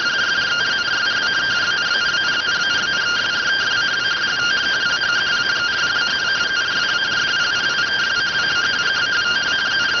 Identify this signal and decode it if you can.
Two-tone teleprinter signal, about 170 Hz shift.